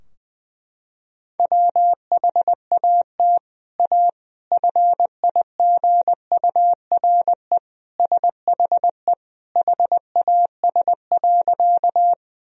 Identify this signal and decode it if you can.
Morse code, heard as a keyed tone, about 20 wpm, 700 Hz. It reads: WHAT A FIGURE SHE HAS.
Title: Sentences from Top 1000 Words - Review Part 2 - 20wpm